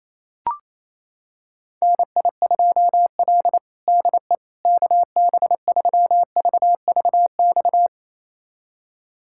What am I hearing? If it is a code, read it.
Morse code, heard as a keyed tone, about 28 wpm, 700 Hz. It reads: NI2L DE KB3VVX
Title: USA Callsign DE Callsign 28wpm